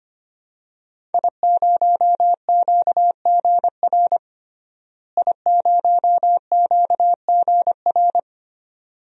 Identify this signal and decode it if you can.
Morse code, heard as a keyed tone, about 25 wpm, 700 Hz. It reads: I0QGR I0QGR
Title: DXCC-100 Call Signs - 5 Characters - Repeated 2x - 25wpm